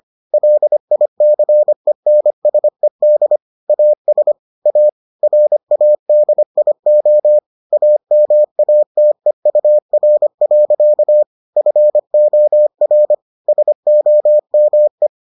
Morse code, heard as a keyed tone, about 25 wpm, 600 Hz. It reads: LICENSED AS A RADIO AMATEUR. FOR SOME